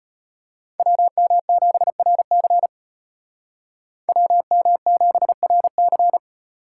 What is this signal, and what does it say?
Morse code, heard as a keyed tone, about 33 wpm, 700 Hz. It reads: WM7RC WM7RC